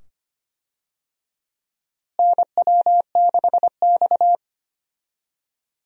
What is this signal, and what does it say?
Morse code, heard as a keyed tone, about 25 wpm, 700 Hz. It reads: NW6X
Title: USA Call Signs - 4 Characters - Repeated 2x - 25wpm